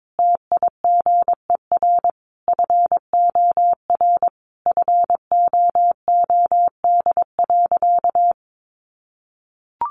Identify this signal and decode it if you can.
Morse code, heard as a keyed tone, about 22 wpm, 700 Hz. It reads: TIGER FOR FOOD.
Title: A Sound Of Thunder 22wpm